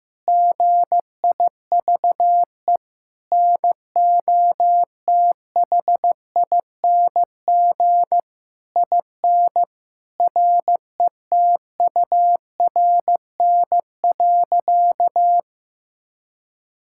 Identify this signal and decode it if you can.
Morse code, heard as a keyed tone, about 15 wpm, 700 Hz. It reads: GIVE NOTHING IN RETURN.
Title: Sentences from Top 1000 Words - Review Part 6 - 15wpm